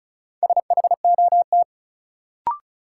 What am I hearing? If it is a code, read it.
Morse code, heard as a keyed tone, about 34 wpm, 700 Hz. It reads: SHOT
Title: Sentences from Top 1100 Words - New Word Drill Alt - 35wpm